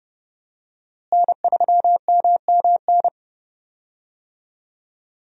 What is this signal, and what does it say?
Morse code, heard as a keyed tone, about 29 wpm, 700 Hz. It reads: N3MMN
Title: USA Call Signs - 5 Characters - Repeated 3x - 30wpm